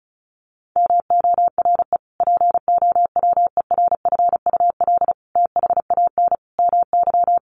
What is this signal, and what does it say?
Morse code, heard as a keyed tone, about 34 wpm, 700 Hz. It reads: MORE POWERFUL THAN MY
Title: Sentences from Top 1300 Words 35wpm